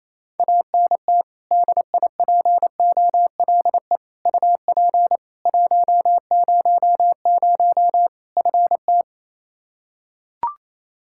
Morse code, heard as a keyed tone, about 28 wpm, 700 Hz. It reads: ANT DIPOLE UP 100 FT
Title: Practice QSO - Exchange 2 - Info - 28wpm